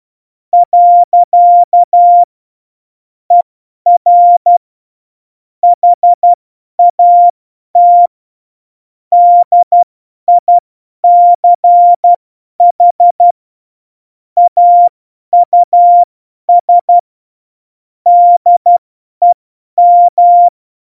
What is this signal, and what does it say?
Morse code, heard as a keyed tone, about 12 wpm, spaced as for 10 wpm, 700 Hz. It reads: . ER HAT DICH AUS DEM
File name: Morsekurs / Witze / Witz 03 April 2018, 50 F70